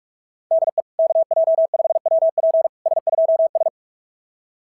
Morse code, heard as a keyed tone, about 44 wpm, 690 Hz. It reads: DE KJ5WP S2S
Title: SOTA QSO Practice 45wpm